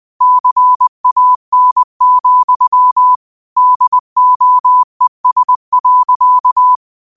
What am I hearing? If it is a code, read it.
Morse code, heard as a keyed tone, about 20 wpm, 1000 Hz. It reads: CAN, DOES.
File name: 20wpm